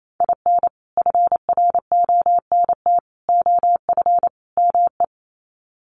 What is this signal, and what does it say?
Morse code, heard as a keyed tone, about 28 wpm, 700 Hz. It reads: IN FRONT OF ME
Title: Sets of 4 Words 28wpm